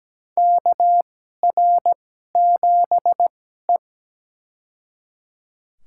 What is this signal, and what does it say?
Morse code, heard as a keyed tone, about 17 wpm, spaced as for 12 wpm, 700 Hz. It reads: KR7E
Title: USA Call Signs - 4 Characters - 2x Character Spacing - 17wpm